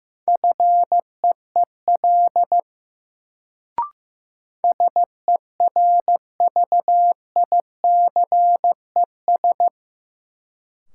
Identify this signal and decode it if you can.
Morse code, heard as a keyed tone, about 15 wpm, 700 Hz. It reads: FEEL SERVICES
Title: Sets of 5 Words - New Word Drill - 15wpm